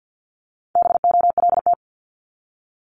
Morse code, heard as a keyed tone, about 50 wpm, 700 Hz. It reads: BOLT
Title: Multi-letter Phonemes Repeated 3x 50wpm